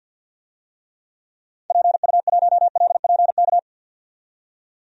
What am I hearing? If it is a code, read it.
Morse code, heard as a keyed tone, about 51 wpm, 700 Hz. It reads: PU1LPK
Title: DXCC-100 Call Signs - 6 Characters - 50wpm